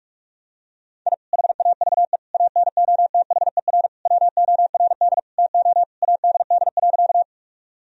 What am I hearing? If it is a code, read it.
Morse code, heard as a keyed tone, about 43 wpm, 700 Hz. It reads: I HAVE ANOTHER WORD TO ADD.
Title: Sentences from Top 1000 Words - Review Part 1 - 45wpm